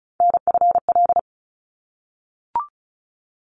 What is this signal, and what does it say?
Morse code, heard as a keyed tone, about 34 wpm, 700 Hz. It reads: NFL
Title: ARRL Sections 35wpm